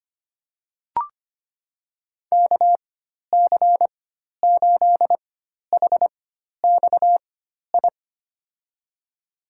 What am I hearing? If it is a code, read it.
Morse code, heard as a keyed tone, about 25 wpm, spaced as for 12 wpm, 700 Hz. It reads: KC8HXI
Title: USA Call Signs - 6 Characters - 4x Character Spacing - 25wpm